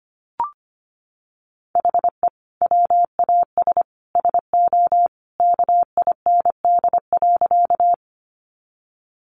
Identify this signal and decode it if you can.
Morse code, heard as a keyed tone, about 25 wpm, 700 Hz. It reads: HE WAS SO KIND.
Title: Sentences from Top 200 Words 25wpm